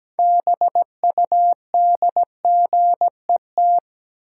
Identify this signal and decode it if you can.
Morse code, heard as a keyed tone, about 17 wpm, 700 Hz. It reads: BUDGET
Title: Six Letter Words - 17wpm